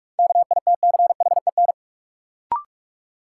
Morse code, heard as a keyed tone, about 42 wpm, 700 Hz. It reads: KITCHEN
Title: Sentences from Top 1400 Words - New Word Drill - 45wpm